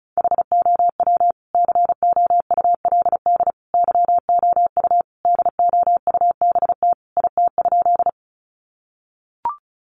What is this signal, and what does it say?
Morse code, heard as a keyed tone, about 34 wpm, 700 Hz. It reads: HOW COULD YOU DOUBT IT?
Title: Sentences from Top 1500 Words 35wpm